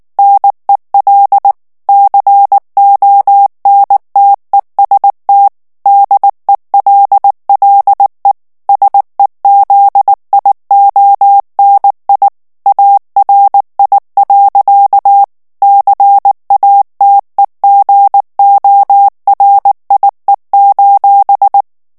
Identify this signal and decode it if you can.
Morse code, heard as a keyed tone, about 19 wpm, 800 Hz. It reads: NEL CONTEST DELLE SEZIONI ARI. CATEGORIE:
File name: morse mp3 / ITALIEN texte clair 2 vitesse 18 mots minute